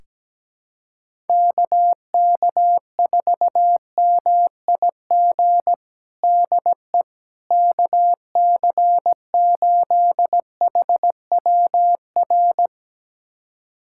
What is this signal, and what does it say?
Morse code, heard as a keyed tone, about 17 wpm, 700 Hz. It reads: KK4MIG DE KC8HWR
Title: USA Callsign DE Callsign 17wpm